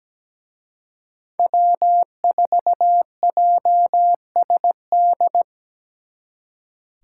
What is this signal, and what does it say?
Morse code, heard as a keyed tone, about 20 wpm, 700 Hz. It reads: W4JSD
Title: Speed Racing - USA 1x3 Call Signs 17wpm